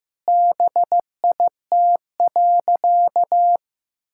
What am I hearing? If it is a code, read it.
Morse code, heard as a keyed tone, about 16 wpm, 700 Hz. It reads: BIT.
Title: Sentences from Top 1000 Words - Review Part 5 - 15wpm